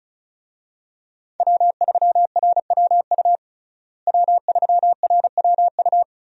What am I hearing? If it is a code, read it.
Morse code, heard as a keyed tone, about 36 wpm, 700 Hz. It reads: W3RWU W3RWU
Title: USA Call Signs - 5 Characters - Repeated 3x - 35wpm